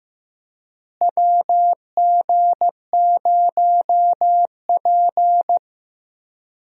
Morse code, heard as a keyed tone, about 16 wpm, 700 Hz. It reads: WG0P
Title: USA Call Signs - 4 Characters - Repeated 3x - 15wpm